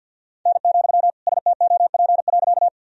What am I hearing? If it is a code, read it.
Morse code, heard as a keyed tone, about 52 wpm, 700 Hz. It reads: N, STOP.
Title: Sentences from Top 300 Words and Abbreviations 50wpm